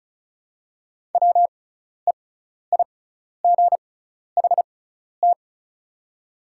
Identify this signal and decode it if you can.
Morse code, heard as a keyed tone, about 36 wpm, spaced as for 12 wpm, 700 Hz. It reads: WEIGHT
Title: 6x Character Spacing - Top 500 Words - 35wpm